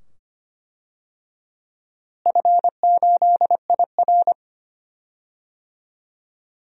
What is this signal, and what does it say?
Morse code, heard as a keyed tone, about 25 wpm, 700 Hz. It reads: F8IR
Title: DXCC-100 Call Signs - 4 Characters - Repeated 2x - 25wpm